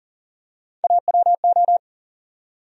Morse code, heard as a keyed tone, about 40 wpm, 700 Hz. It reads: AWO
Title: USA Call Sign Postfixes - 3 Letters 40wpm